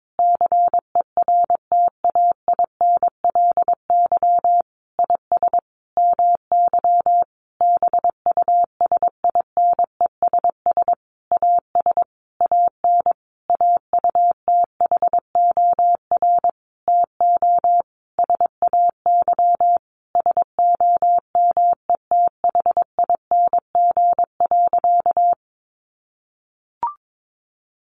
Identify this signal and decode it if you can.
Morse code, heard as a keyed tone, about 22 wpm, 700 Hz. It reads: CERTAINLY IS MY BUSINESS AS AN AUTHOR TO SAY SOMETHING.
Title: Sentences from Top 1200 Words 22wpm